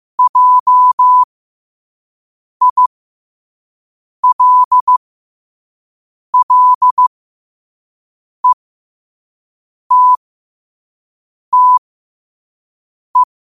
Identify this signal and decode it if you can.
Morse code, heard as a keyed tone, about 15 wpm, spaced as for 5 wpm, 1000 Hz. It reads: JILLETTE
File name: 5wpm